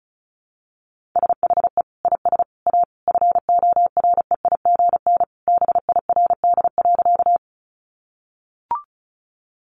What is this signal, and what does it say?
Morse code, heard as a keyed tone, about 34 wpm, 700 Hz. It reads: SHE IS A FOREIGN BIRD.
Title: Sentences from Top 1400 Words 35wpm